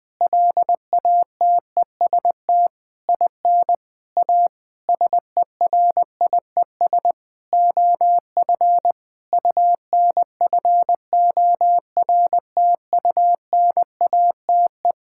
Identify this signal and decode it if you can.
Morse code, heard as a keyed tone, about 21 wpm, 700 Hz. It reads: LATEST IN A SERIES OF UNFORTUNATE
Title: Sentences with 8-Word Phrases - Spoken Beginning - 20wpm